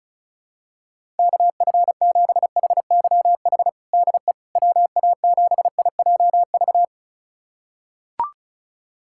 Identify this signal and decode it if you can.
Morse code, heard as a keyed tone, about 34 wpm, 690 Hz. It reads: KF7HYH DE WA7IJV
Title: USA Callsign DE Callsign - 6 Characters - 35wpm